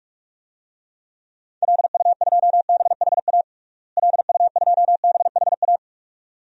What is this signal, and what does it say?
Morse code, heard as a keyed tone, about 41 wpm, 700 Hz. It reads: LU2BHA LU2BHA
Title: DXCC-100 Call Signs - 6 Characters - Repeated 2x - 45wpm